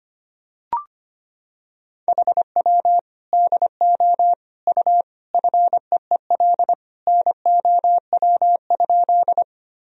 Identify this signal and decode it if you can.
Morse code, heard as a keyed tone, about 25 wpm, 700 Hz. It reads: HW DO U FEEL NOW?
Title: Sentences from Top 400 Words and Abbreviations 25wpm